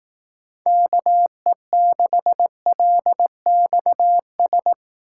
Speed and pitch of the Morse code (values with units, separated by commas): 19 wpm, 700 Hz